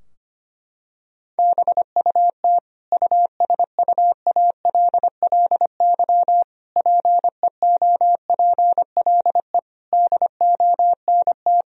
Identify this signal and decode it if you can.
Morse code, heard as a keyed tone, about 25 wpm, 700 Hz. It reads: BUT USUALLY PEOPLE DONT
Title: Sentences from Top 1000 Words - Review Part 3 - 25wpm